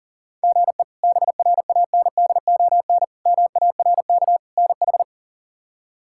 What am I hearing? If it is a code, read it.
Morse code, heard as a keyed tone, about 40 wpm, 700 Hz. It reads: GE BRANDON MARK NH
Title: K1USN SST QSO Practice - 40wpm